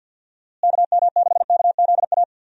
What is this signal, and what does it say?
Morse code, heard as a keyed tone, about 51 wpm, 700 Hz. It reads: KM6KZA
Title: USA Call Signs - 6 Characters - Repeated 2x - 50wpm